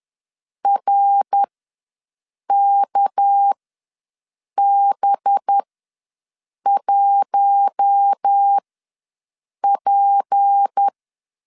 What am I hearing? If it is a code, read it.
Morse code, heard as a keyed tone, about 10 wpm, spaced as for 6 wpm, 800 Hz. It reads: RKB1P
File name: LEZIONI / 6 PKX0 / 6 PK ws 10wpm 3745